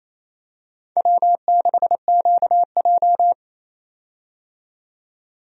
Morse code, heard as a keyed tone, about 28 wpm, 700 Hz. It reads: W6QJ
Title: USA Call Signs - 4 Characters - 28wpm